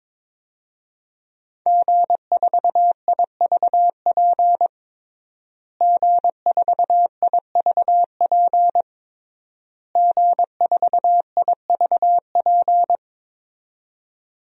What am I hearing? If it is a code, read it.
Morse code, heard as a keyed tone, about 22 wpm, 700 Hz. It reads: G4IVP G4IVP G4IVP
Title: DXCC-100 Call Signs - 5 Characters - Repeated 3x - 22wpm